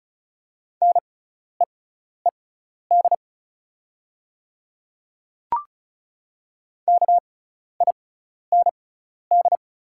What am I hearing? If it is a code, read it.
Morse code, heard as a keyed tone, about 34 wpm, spaced as for 12 wpm, 700 Hz. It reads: NEED KIND
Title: 6x Character Spacing - Top 200 Words - 35wpm